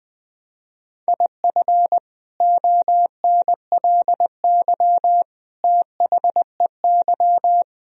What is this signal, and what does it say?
Morse code, heard as a keyed tone, about 20 wpm, 700 Hz. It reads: IF ONLY THEY
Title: Sentences from Top 500 Words and Abbreviations 20wpm